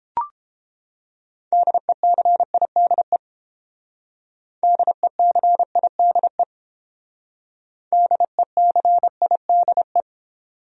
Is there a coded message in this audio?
Yes: Morse code from 1.5 to 10.0 s, heard as a keyed tone, about 29 wpm, 700 Hz. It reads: DECIDE DECIDE DECIDE